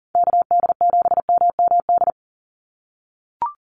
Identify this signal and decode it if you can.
Morse code, heard as a keyed tone, about 40 wpm, 700 Hz. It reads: KD7MMD
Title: USA Call Signs 40wpm